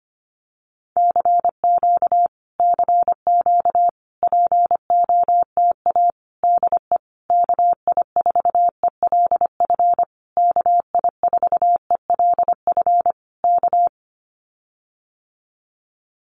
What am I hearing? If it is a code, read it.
Morse code, heard as a keyed tone, about 25 wpm, 700 Hz. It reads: CQ CQ POTA DE KI4ELF KI4ELF K
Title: Common QSO Phrases Encore 25wpm